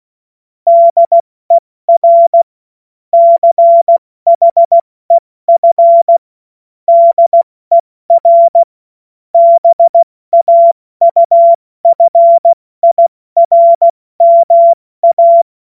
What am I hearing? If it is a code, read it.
Morse code, heard as a keyed tone, about 16 wpm, 680 Hz. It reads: DER CHEF DER BAUFIRMA
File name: Morsekurs / Witze / Witz 13. Januar 2026 70 F80